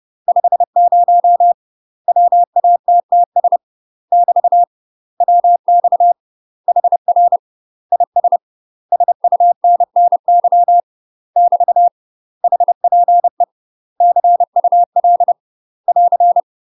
Morse code, heard as a keyed tone, about 30 wpm, 700 Hz. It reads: 50 WATTS <BT> WX HR IS SUNNY <BT> HPE CUL <AR>
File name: UK MASTER QSOs TRAINING / 30WPM / UK MASTER QSOs 0030